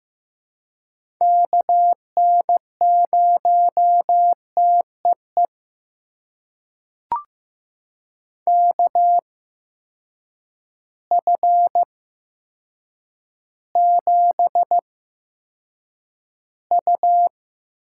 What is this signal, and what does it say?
Morse code, heard as a keyed tone, about 15 wpm, spaced as for 7 wpm, 700 Hz. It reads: KN0TEE KF7U
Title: USA Call Signs - 6 Characters - 8x Character Spacing - 15wpm